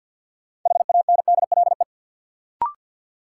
Morse code, heard as a keyed tone, about 46 wpm, 700 Hz. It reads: HANDLE
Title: Sentences from Top 1100 Words - New Word Drill Alt - 50wpm